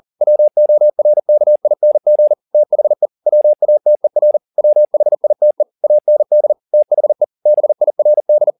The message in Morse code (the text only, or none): WORKING THE WATER WHITE AND THE BIRD